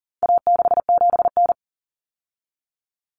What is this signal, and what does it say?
Morse code, heard as a keyed tone, about 40 wpm, 700 Hz. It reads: A67N
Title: Common Cut Numbers - 4-Digit 40wpm